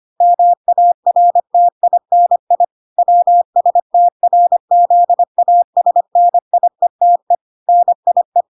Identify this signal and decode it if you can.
Morse code, heard as a keyed tone, about 25 wpm, 700 Hz. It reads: MARTINI WSTRZASNIETE NIE